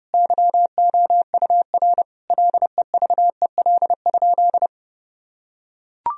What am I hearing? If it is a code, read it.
Morse code, heard as a keyed tone, about 32 wpm, 700 Hz. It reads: YOUR LEVEL?
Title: Sentences from Top 1000 Words - Review Part 4 - 30wpm